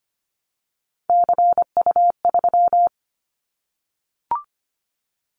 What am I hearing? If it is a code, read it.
Morse code, heard as a keyed tone, about 25 wpm, 700 Hz. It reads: CU3
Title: Callsign Prefixes for Top 100 DXCC - 25wpm